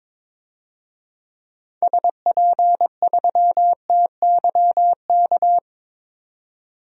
Morse code, heard as a keyed tone, about 22 wpm, 700 Hz. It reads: SP3TYK
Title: DXCC-100 Call Signs - 6 Characters - 22wpm